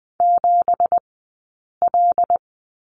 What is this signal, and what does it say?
Morse code, heard as a keyed tone, about 20 wpm, spaced as for 8 wpm, 700 Hz. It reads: 7L
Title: 197 Lesson 32d - Letter k - Two Characters 20wpm